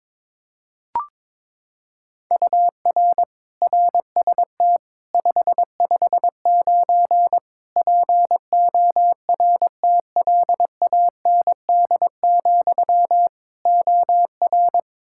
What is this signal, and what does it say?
Morse code, heard as a keyed tone, about 22 wpm, 700 Hz. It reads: UR RST 559 PORTLAND, OR
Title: K1USN SST QSO Practice 22wpm